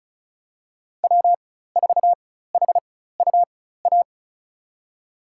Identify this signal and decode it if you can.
Morse code, heard as a keyed tone, about 34 wpm, spaced as for 16 wpm, 700 Hz. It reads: W4HUA
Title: USA Call Signs - 5 Characters - 4x Character Spacing - 35wpm